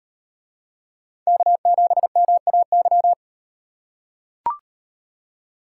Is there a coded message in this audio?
Yes: Morse code from 1.3 to 3.1 s, heard as a keyed tone, about 38 wpm, 700 Hz. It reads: K7MAY